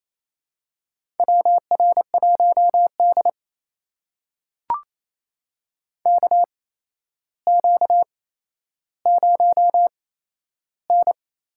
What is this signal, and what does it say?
Morse code, heard as a keyed tone, about 28 wpm, spaced as for 12 wpm, 700 Hz. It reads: WR1D KQ0N
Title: USA Call Signs - 4 Characters - 8x Character Spacing - 28wpm